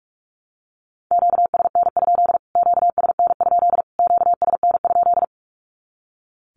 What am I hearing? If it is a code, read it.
Morse code, heard as a keyed tone, about 45 wpm, 700 Hz. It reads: QSN? QSN? QSN?
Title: Q-Codes Repeated 3x 45wpm